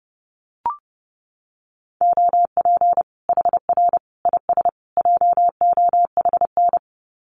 Morse code, heard as a keyed tone, about 30 wpm, 700 Hz. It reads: OP HR IS JOHN
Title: Common QSO Phrases 30wpm